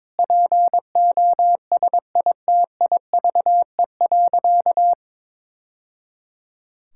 Morse code, heard as a keyed tone, about 22 wpm, 700 Hz. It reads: POSITIVE.
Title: Sentences from Top 1000 Words - Review Part 5 - 22wpm